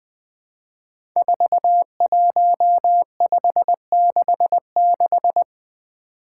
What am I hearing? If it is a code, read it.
Morse code, heard as a keyed tone, about 20 wpm, 700 Hz. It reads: 41566
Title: Five Challenging Letter-Number Combinations 20wpm